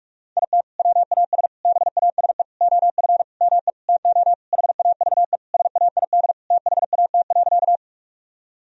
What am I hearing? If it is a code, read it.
Morse code, heard as a keyed tone, about 46 wpm, 700 Hz. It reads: IT WAS BASE OF ME TO HAVE SAID THAT.
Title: Sentences from Top 1000 Words - Review Part 1 - 45wpm